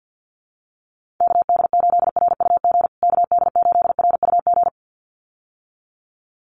Morse code, heard as a keyed tone, about 50 wpm, 700 Hz. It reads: KD8RUG KD8RUG
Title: USA Call Signs Repeated 2x 50wpm